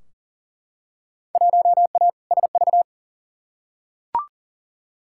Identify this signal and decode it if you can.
Morse code, heard as a keyed tone, about 40 wpm, 700 Hz. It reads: 1A SV
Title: Field Day Exchanges - 40wpm